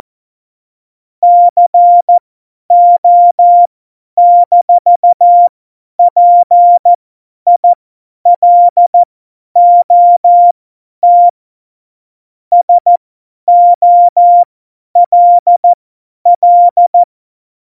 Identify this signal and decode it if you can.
Morse code, heard as a keyed tone, about 14 wpm, spaced as for 10 wpm, 700 Hz. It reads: CO-PILOT SOLL
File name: Morsekurs / Witze / Witz 22. Dezember 2020, 50 F70